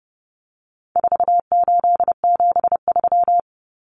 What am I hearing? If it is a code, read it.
Morse code, heard as a keyed tone, about 30 wpm, 700 Hz. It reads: 4873
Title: Common Cut Numbers - 4-Digit 30wpm